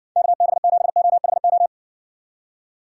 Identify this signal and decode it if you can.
Morse code, heard as a keyed tone, about 70 wpm, 700 Hz. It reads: KB7OHO